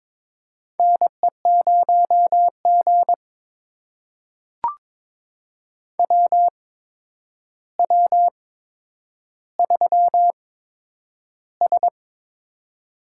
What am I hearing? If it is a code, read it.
Morse code, heard as a keyed tone, about 23 wpm, spaced as for 10 wpm, 700 Hz. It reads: NE0G WW3S
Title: USA Call Signs - 4 Characters - 8x Character Spacing - 22wpm